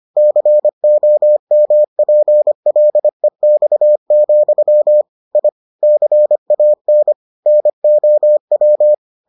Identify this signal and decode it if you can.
Morse code, heard as a keyed tone, about 25 wpm, 600 Hz. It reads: COMPLEX, I CAN NOW